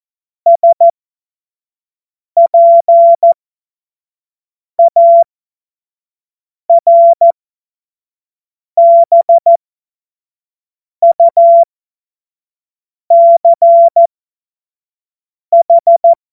Morse code, heard as a keyed tone, about 14 wpm, spaced as for 5 wpm, 680 Hz. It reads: SPARBUCH
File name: Morsekurs / Witze / Witz 20. Dezember 2022 25 F70